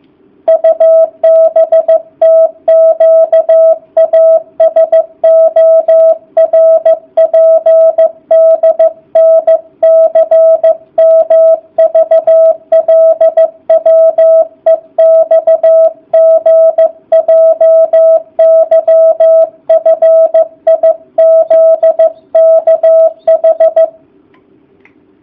Morse code, heard as a keyed tone, about 14 wpm, 650 Hz. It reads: UBTQASORPDNCMVLWEXGJYFIZKH